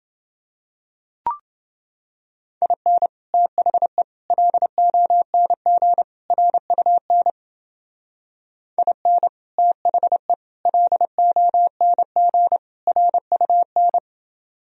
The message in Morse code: IN THE LONG RUN IN THE LONG RUN

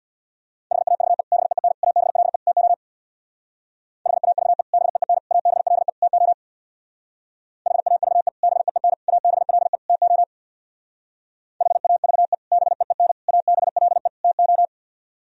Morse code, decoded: HAVE BEEN ABLE TO HAVE BEEN ABLE TO HAVE BEEN ABLE TO HAVE BEEN ABLE TO

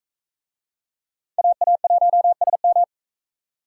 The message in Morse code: AA1SM

